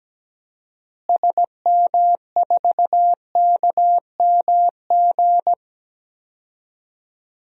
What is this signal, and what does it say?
Morse code, heard as a keyed tone, about 17 wpm, 700 Hz. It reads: SM4KMG
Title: DXCC-100 Call Signs - 6 Characters - Repeated 3x - 17wpm